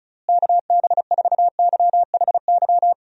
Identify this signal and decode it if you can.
Morse code, heard as a keyed tone, about 34 wpm, 700 Hz. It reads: KB4YHY
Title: USA Call Signs - 6 Characters - Repeated 3x - 35wpm